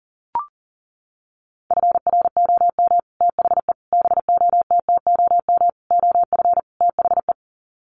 Morse code, heard as a keyed tone, about 40 wpm, 700 Hz. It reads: FROM THE BOTTOM OF THE
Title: Sets of 5 Words Encore2 40wpm